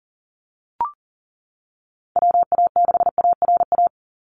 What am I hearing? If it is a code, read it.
Morse code, heard as a keyed tone, about 40 wpm, 700 Hz. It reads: WA6ARA
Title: NA SOTA Chaser Callsigns 40wpm